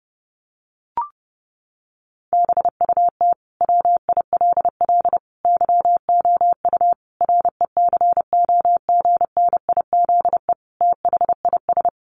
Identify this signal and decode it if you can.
Morse code, heard as a keyed tone, about 30 wpm, 700 Hz. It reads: BUT WILL YOU RECOGNIZE THIS
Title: Sentences from Top 1400 Words 30wpm